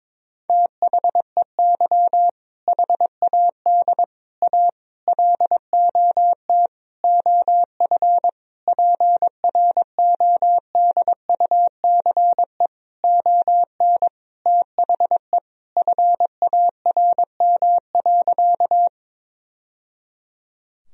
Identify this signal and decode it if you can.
Morse code, heard as a keyed tone, about 22 wpm, 700 Hz. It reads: THEY HAD A LOT OF PRODUCE ON THE FARM.
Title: Sentences from Top 1000 Words - Review Part 3 - 22wpm